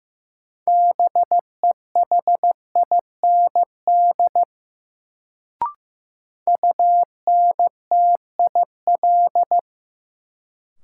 Morse code, heard as a keyed tone, about 15 wpm, 700 Hz. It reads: BEHIND UNTIL
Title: Sentences from Top 500 Words - New Word Drill - 15wpm